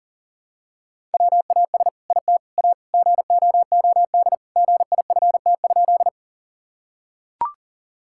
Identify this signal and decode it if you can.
Morse code, heard as a keyed tone, about 37 wpm, 700 Hz. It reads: WAS IT A GOOD GIFT?
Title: Sentences from Top 1000 Words - Review Part 6 - 40wpm